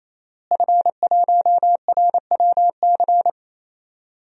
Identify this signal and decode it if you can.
Morse code, heard as a keyed tone, about 28 wpm, 700 Hz. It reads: F1RWC
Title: DXCC-100 Call Signs - 5 Characters - Repeated 3x - 28wpm